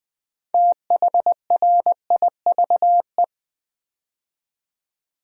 Morse code, heard as a keyed tone, about 20 wpm, 700 Hz. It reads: THRIVE
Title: Build A Word - 2 Letters At A Time - 6-Letter Words - 20wpm